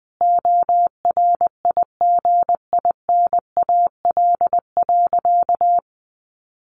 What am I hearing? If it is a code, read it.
Morse code, heard as a keyed tone, about 20 wpm, 700 Hz. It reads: ORIGINAL.
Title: Sentences from Top 900 Words 20wpm